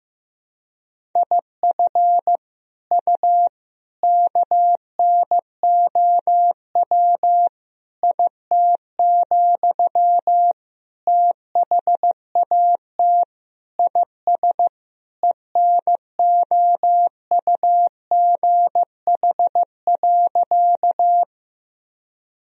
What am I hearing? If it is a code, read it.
Morse code, heard as a keyed tone, about 15 wpm, 700 Hz. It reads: IF U KNOW IT, THAT IS ENOUGH.
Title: Sentences from Top 400 Words and Abbreviations 15wpm